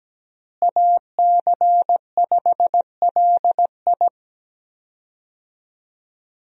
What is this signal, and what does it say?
Morse code, heard as a keyed tone, about 17 wpm, 700 Hz. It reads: AC5LI
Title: USA Call Signs - 5 Characters - Repeated 2x - 17wpm